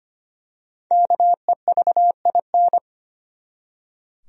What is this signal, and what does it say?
Morse code, heard as a keyed tone, about 25 wpm, 700 Hz. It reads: KEVIN
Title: USA Common Names - Max 5 Letters - 25wpm